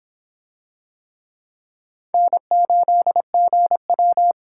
Morse code, heard as a keyed tone, about 25 wpm, 700 Hz. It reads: N8GW